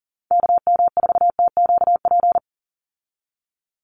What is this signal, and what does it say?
Morse code, heard as a keyed tone, about 40 wpm, 700 Hz. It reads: KM4TQP
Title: USA Call Signs 40wpm